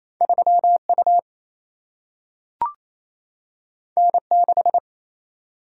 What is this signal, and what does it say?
Morse code, heard as a keyed tone, about 28 wpm, 700 Hz. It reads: 3 U N 6
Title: Single-Letter Single-Number 28wpm